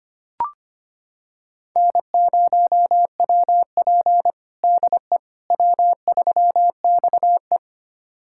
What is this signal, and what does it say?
Morse code, heard as a keyed tone, about 25 wpm, 700 Hz. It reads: N0WP DE W3XE
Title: USA Callsign DE Callsign - 4 Characters - 25wpm